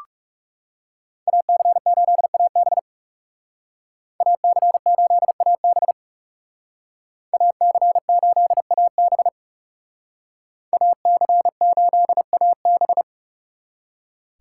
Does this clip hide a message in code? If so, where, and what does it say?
Yes: Morse code from 1.3 to 13.0 s, heard as a keyed tone, about 40 wpm, 700 Hz. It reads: AC8AB AC8AB AC8AB AC8AB